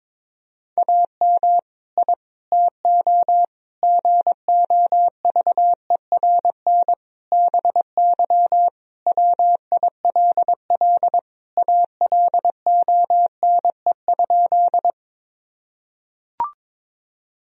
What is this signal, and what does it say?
Morse code, heard as a keyed tone, about 22 wpm, 700 Hz. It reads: AM I TO GOVERN BY WILL ALONE?
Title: Sentences from Top 1000 Words - Review Part 2 - 22wpm